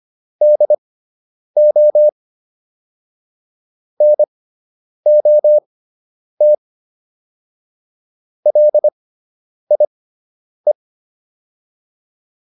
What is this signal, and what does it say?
Morse code, heard as a keyed tone, about 25 wpm, spaced as for 9 wpm, 600 Hz. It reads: DO NOT LIE